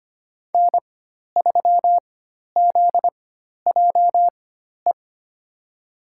Morse code, heard as a keyed tone, about 25 wpm, spaced as for 12 wpm, 700 Hz. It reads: N3ZJE
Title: USA Call Signs - 5 Characters - 4x Character Spacing - 25wpm